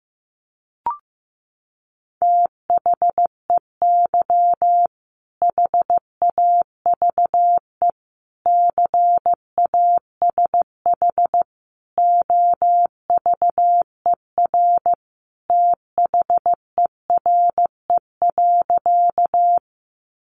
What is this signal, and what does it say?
Morse code, heard as a keyed tone, about 15 wpm, 700 Hz. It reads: THEY HAVE CASH OVER THERE.
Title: Sentences from Top 700 Words 15wpm